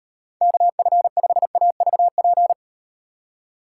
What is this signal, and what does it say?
Morse code, heard as a keyed tone, about 38 wpm, 700 Hz. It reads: KF5AVP